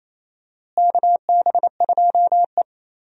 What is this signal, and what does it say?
Morse code, heard as a keyed tone, about 28 wpm, 700 Hz. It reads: KB2E
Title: USA Call Signs - 4 Characters - Repeated 3x - 28wpm